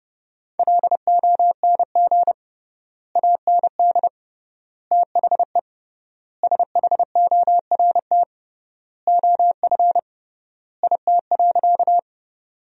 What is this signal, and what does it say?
Morse code, heard as a keyed tone, about 29 wpm, 700 Hz. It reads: LONG AND THE SHORT OF IT.
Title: 3x Word-Spacing - Sentences from Top 400 Words - 30wpm